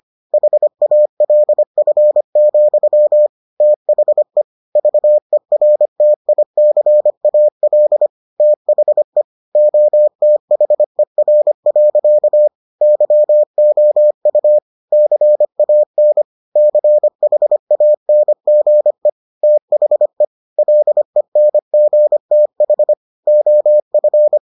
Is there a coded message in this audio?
Yes: Morse code from 0.3 to 24.4 s, heard as a keyed tone, about 25 wpm, 600 Hz. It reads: HALF, THE VERTICAL THE OTHER. YOU CAN CHANGE THE LENGTH OF